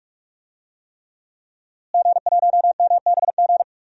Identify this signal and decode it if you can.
Morse code, heard as a keyed tone, about 44 wpm, 700 Hz. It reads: G1MBG
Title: DXCC-100 Call Signs - 5 Characters - Repeated 3x - 45wpm